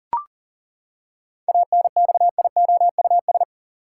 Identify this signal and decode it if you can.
Morse code, heard as a keyed tone, about 40 wpm, 700 Hz. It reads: ANXIOUS
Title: Seven Letter Words - 40wpm